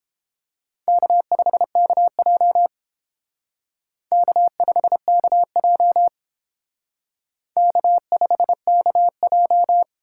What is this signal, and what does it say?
Morse code, heard as a keyed tone, about 29 wpm, 700 Hz. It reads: K5KJ K5KJ K5KJ